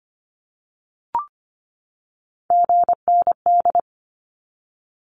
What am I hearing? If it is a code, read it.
Morse code, heard as a keyed tone, about 25 wpm, 700 Hz. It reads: GND
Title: Abbreviations 25wpm